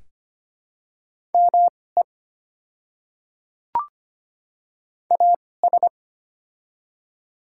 Morse code, heard as a keyed tone, about 25 wpm, spaced as for 18 wpm, 700 Hz. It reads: ME AS
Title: 2x Character Spacing - Top 200 Words - 25wpm